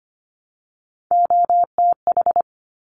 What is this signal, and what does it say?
Morse code, heard as a keyed tone, about 25 wpm, 700 Hz. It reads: OTH
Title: Top Three Letter Combinations in Words 25wpm